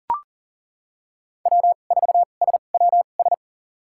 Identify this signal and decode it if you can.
Morse code, heard as a keyed tone, about 40 wpm, spaced as for 29 wpm, 700 Hz. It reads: W4SWS
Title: USA Call Signs - 5 Characters - 2x Character Spacing - 40wpm